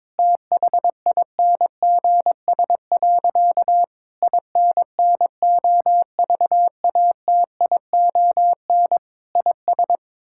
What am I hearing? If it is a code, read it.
Morse code, heard as a keyed tone, about 22 wpm, 700 Hz. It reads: THINGS. INNOVATION IS
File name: Difficult Sentences with Hints - 22wpm